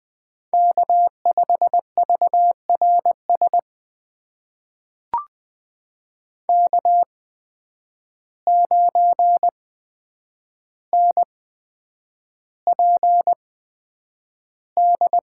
Morse code, heard as a keyed tone, about 20 wpm, spaced as for 9 wpm, 700 Hz. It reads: K5VRS K9NPD
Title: USA Call Signs - 5 Characters - 8x Character Spacing - 20wpm